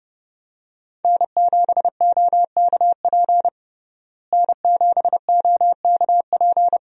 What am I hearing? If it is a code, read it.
Morse code, heard as a keyed tone, about 30 wpm, 700 Hz. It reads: N7OKP N7OKP
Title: USA Call Signs - 5 Characters - Repeated 2x - 30wpm